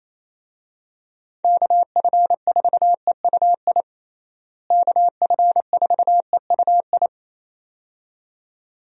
Morse code, heard as a keyed tone, about 28 wpm, 700 Hz. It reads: KF4EUI KF4EUI
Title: USA Call Signs - 6 Characters - Repeated 2x - 28wpm